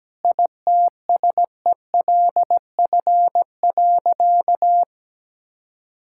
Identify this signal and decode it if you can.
Morse code, heard as a keyed tone, about 17 wpm, 700 Hz. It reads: ITSELF.
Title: Sentences from Top 1000 Words - Review Part 5 - 17wpm